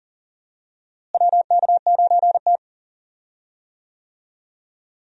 Morse code, heard as a keyed tone, about 40 wpm, 700 Hz. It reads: WK9T
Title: USA Call Signs - 4 Characters - Repeated 2x - 40wpm